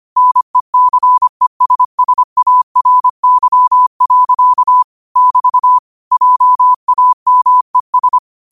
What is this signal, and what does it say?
Morse code, heard as a keyed tone, about 25 wpm, 1000 Hz. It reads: NECESSARY. <BT> JAMES
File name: qotdfiles / 25wpm